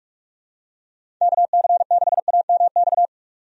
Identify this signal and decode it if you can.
Morse code, heard as a keyed tone, about 43 wpm, 700 Hz. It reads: KC6AMX
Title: USA Call Signs - 6 Characters - 45wpm